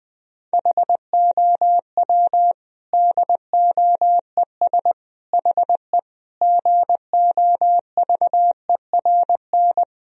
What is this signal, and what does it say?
Morse code, heard as a keyed tone, about 21 wpm, 700 Hz. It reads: HOW DOES HE GOVERN
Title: Sentences from Top 1000 Words - Review Part 2 - 20wpm